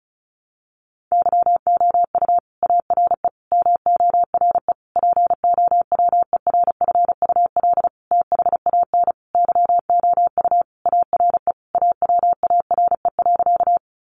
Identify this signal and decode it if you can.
Morse code, heard as a keyed tone, about 34 wpm, 700 Hz. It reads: YOU ARE MORE POWERFUL THAN YOU ARE AWARE.
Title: Sentences from Top 1300 Words 35wpm